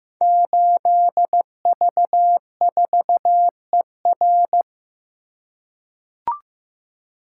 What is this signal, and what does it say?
Morse code, heard as a keyed tone, about 15 wpm, 700 Hz. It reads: 8V4ER
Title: Code Groups - 5 Letter:Numbers - 15wpm